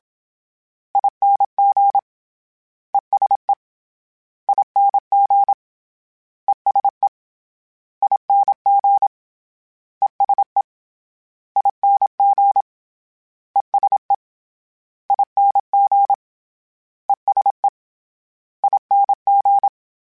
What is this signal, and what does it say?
Morse code, heard as a keyed tone, about 27 wpm, 800 Hz. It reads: ING ESE ING ESE ING ESE ING ESE ING ESE ING